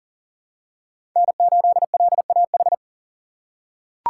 Morse code, heard as a keyed tone, about 40 wpm, 700 Hz. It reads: N8LAH
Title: USA Call Signs - 5 Characters - 6x Character Spacing - 40wpm